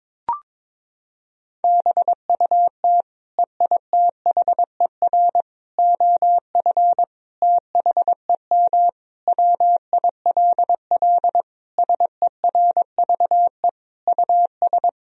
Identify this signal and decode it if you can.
Morse code, heard as a keyed tone, about 23 wpm, 700 Hz. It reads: BUT EITHER OF THEM WILL SERVE US
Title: Sentences from Top 1000 Words - Review Part 4 - 22wpm